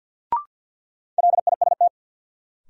Morse code, heard as a keyed tone, about 52 wpm, 700 Hz. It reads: LIST
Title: Sentences from Top 400 Words - New Word Drill - 50wpm